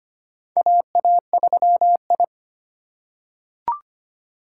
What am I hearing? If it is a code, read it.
Morse code, heard as a keyed tone, about 25 wpm, 700 Hz. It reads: AA3I
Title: USA Call Signs - 4 Characters - 2x Character Spacing - 25wpm